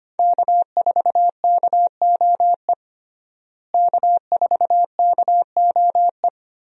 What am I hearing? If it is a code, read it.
Morse code, heard as a keyed tone, about 25 wpm, 700 Hz. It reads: K4KOE K4KOE
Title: USA Call Signs - 5 Characters - Repeated 3x - 25wpm